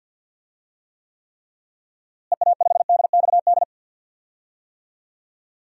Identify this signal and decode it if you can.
Morse code, heard as a keyed tone, about 48 wpm, 700 Hz. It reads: EA5DXD
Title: DXCC-100 Call Signs - 6 Characters - Repeated 3x - 50wpm